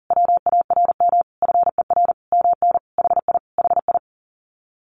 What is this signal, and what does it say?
Morse code, heard as a keyed tone, about 40 wpm, 700 Hz. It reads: WARM FER MN HI HI
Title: Common QSO Phrases Encore 40wpm